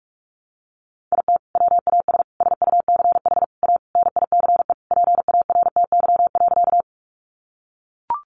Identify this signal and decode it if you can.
Morse code, heard as a keyed tone, about 44 wpm, 700 Hz. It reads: IT WAS SUCH A NICE PARTY.
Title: Sentences from Top 1400 Words 45wpm